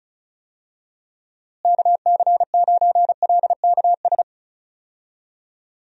Morse code, heard as a keyed tone, about 34 wpm, 700 Hz. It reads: KC9LKS
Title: USA Call Signs - 6 Characters - Repeated 3x - 35wpm